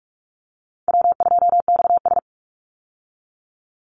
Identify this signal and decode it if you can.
Morse code, heard as a keyed tone, about 44 wpm, 700 Hz. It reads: W2XS
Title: Easy Call Signs Repeated 3x - 45wpm